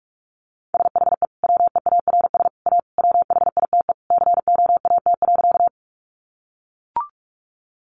Morse code, heard as a keyed tone, about 44 wpm, 700 Hz. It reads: SHE WEARS A WHITE COAT.
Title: Additional Easy Phrases 45wpm